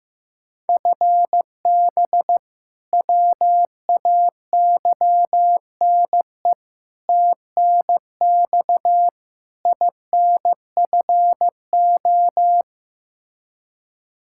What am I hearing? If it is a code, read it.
Morse code, heard as a keyed tone, about 16 wpm, 700 Hz. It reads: FB WAYNE TNX INFO
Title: Practice QSO - Exchange 3 - More Info - 15wpm